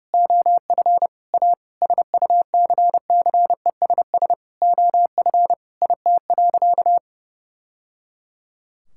Morse code, heard as a keyed tone, about 30 wpm, 700 Hz. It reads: OF A SUCCESS OF IT.
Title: Sentences from Top 1000 Words - Review Part 6 - 30wpm